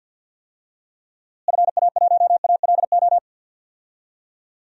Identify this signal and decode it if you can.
Morse code, heard as a keyed tone, about 53 wpm, 700 Hz. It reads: FA1ALO
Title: DXCC-100 Call Signs - 6 Characters - Repeated 2x - 50wpm